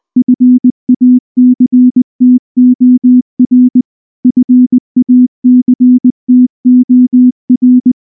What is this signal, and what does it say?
Morse code, heard as a keyed tone, about 20 wpm, 250 Hz. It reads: FACTOR FACTOR